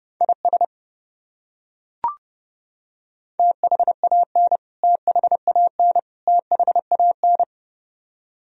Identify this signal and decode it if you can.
Morse code, heard as a keyed tone, about 31 wpm, 700 Hz. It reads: IS THAN THAN THAN
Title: Top 100 Words Repeated 30wpm